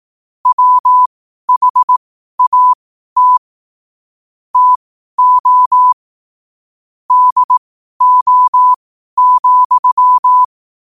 Morse code, heard as a keyed tone, about 18 wpm, spaced as for 12 wpm, 1000 Hz. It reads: WHAT TO DO,
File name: qotdfiles / 12wpm